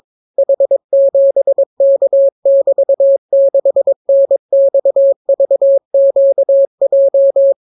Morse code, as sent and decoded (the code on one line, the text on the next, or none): .... --... -.- -...- -.... -. -..- ...- --.- .---
H7K<BT>6NXVQJ